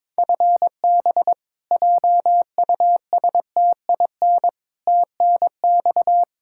Morse code, heard as a keyed tone, about 21 wpm, 700 Hz. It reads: FB JUSTIN TNX
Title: Practice QSO - Exchange 3 - More Info - 22wpm